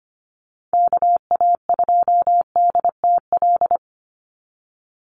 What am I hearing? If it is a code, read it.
Morse code, heard as a keyed tone, about 25 wpm, 700 Hz. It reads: KA2DTL
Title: USA Call Signs 25wpm